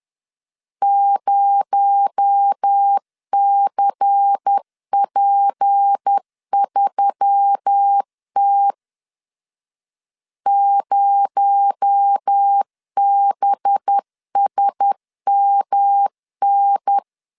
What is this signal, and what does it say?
Morse code, heard as a keyed tone, about 10 wpm, 800 Hz. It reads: 0CP3T 0BSMN
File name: LEZIONI / 6 PKX0 / 6 PKX0 10wpm 2707